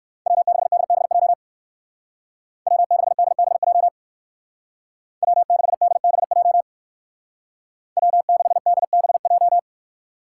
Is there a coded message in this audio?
Yes: Morse code from 0.3 to 9.6 s, heard as a keyed tone, about 59 wpm, 700 Hz. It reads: W6DBJ W6DBJ W6DBJ W6DBJ